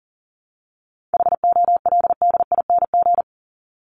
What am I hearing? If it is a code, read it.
Morse code, heard as a keyed tone, about 40 wpm, 700 Hz. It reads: HOLDING
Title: ING Suffix - Top 100 Words with Base Words 40wpm